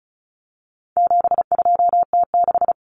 Morse code, heard as a keyed tone, about 34 wpm, 700 Hz. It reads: 72T6
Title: Common Cut Numbers - 4-Digit 35wpm